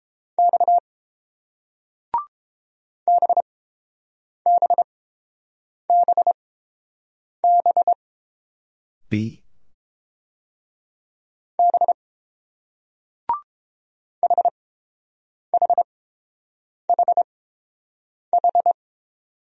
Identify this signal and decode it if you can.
Morse code, heard as a keyed tone, about 30 wpm, spaced as for 7 wpm, 700 Hz. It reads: X BBBB B HHHH